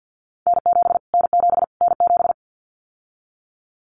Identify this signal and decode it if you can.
Morse code, heard as a keyed tone, about 50 wpm, 700 Hz. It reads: N7 N7 N7
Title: Callsign Prefixes for Top 25 DXCC - Repeated 3x - 50wpm